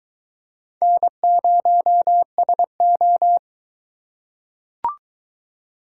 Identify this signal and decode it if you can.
Morse code, heard as a keyed tone, about 21 wpm, 700 Hz. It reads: N0SO